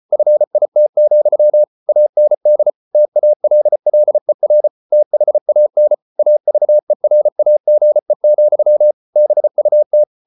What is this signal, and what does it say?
Morse code, heard as a keyed tone, about 34 wpm, 600 Hz. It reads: FIT, AND TALLER THAN AVERAGE, BUT